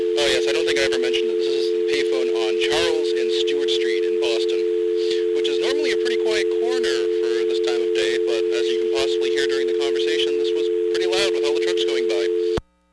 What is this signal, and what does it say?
Dial tone (North America), 350 + 440 Hz, with speech.
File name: blackratchet / recordings / charles-st-payphone